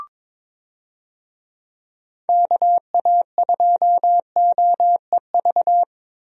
Morse code, heard as a keyed tone, about 22 wpm, 700 Hz. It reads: KA2OEV
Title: USA Call Signs - 6 Characters - Repeated 2x - 22wpm